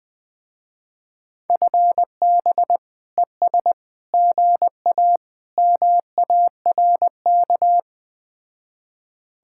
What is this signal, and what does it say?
Morse code, heard as a keyed tone, about 20 wpm, 700 Hz. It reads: FB ES GA MARK
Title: Speed-Racing Common QSO Phrases - 17wpm